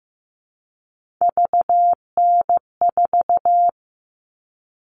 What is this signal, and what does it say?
Morse code, heard as a keyed tone, about 15 wpm, 700 Hz. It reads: VN4
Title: All Cut Numbers - 3-Digit 15wpm